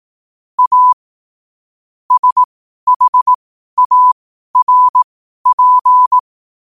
Morse code, heard as a keyed tone, about 18 wpm, spaced as for 12 wpm, 1000 Hz. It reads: A SHARP
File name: qotdfiles / 12wpm